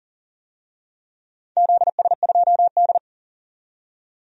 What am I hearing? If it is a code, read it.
Morse code, heard as a keyed tone, about 39 wpm, 700 Hz. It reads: ZS2D
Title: DXCC-100 Call Signs - 4 Characters - 40wpm